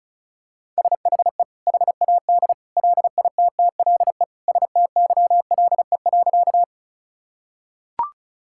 Morse code, heard as a keyed tone, about 34 wpm, 700 Hz. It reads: SHE HAD LITTLE STYLE.
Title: Sentences from Top 1000 Words - Review Part 5 - 35wpm